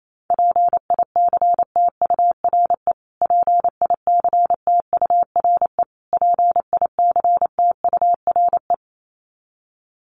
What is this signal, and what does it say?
Morse code, heard as a keyed tone, about 28 wpm, 700 Hz. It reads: PICTURE PICTURE PICTURE
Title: Top 300 Words Repeated 28wpm